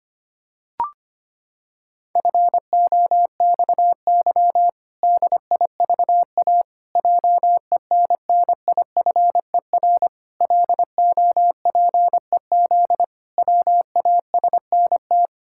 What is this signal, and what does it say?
Morse code, heard as a keyed tone, about 25 wpm, 700 Hz. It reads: FOXY DIVA JENNIFER LOPEZ WASNT
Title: Pangram Challenge v2 - 25wpm